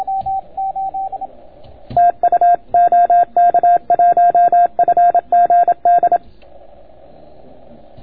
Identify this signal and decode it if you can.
Morse code, heard as a keyed tone, about 27 wpm, 700 Hz. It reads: TU OK1FGD